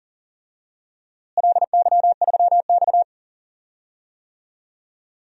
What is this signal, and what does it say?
Morse code, heard as a keyed tone, about 40 wpm, 700 Hz. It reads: LY3X
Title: DXCC-100 Call Signs - 4 Characters - Repeated 3x - 40wpm